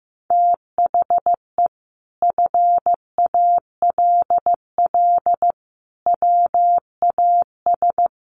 Morse code, heard as a keyed tone, about 15 wpm, 700 Hz. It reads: THE FALL WAS
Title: Sentences from Top 1400 Words 15wpm